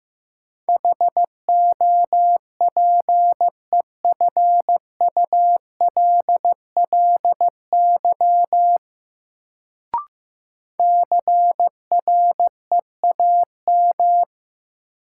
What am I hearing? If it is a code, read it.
Morse code, heard as a keyed tone, about 15 wpm, 700 Hz. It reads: HOPEFULLY CREAM
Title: Sentences from Top 1500 Words - New Word Drill - 15wpm